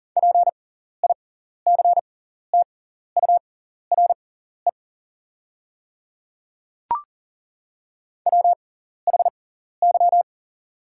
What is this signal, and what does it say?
Morse code, heard as a keyed tone, about 40 wpm, spaced as for 14 wpm, 690 Hz. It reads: PICTURE WHY
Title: 6x Character Spacing - Top 300 Words - 40wpm